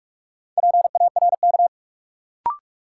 Morse code, heard as a keyed tone, about 47 wpm, 700 Hz. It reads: PARK
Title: Sentences from Top 1300 Words - New Word Drill Alt - 45wpm